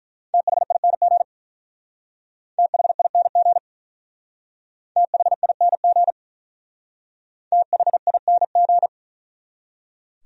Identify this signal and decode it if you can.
Morse code, heard as a keyed tone, about 42 wpm, 700 Hz. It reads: THING THING THING THING
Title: Speed-Racing Top 500 Words 35wpm